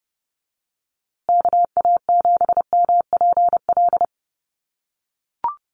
Morse code, heard as a keyed tone, about 30 wpm, 700 Hz. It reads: KA7MPL
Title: USA Call Signs Repeated 2x 30wpm